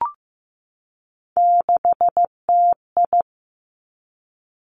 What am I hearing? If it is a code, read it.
Morse code, heard as a keyed tone, about 15 wpm, 700 Hz. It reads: 6TI
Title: Three Challenging Letter-Number Combinations 15wpm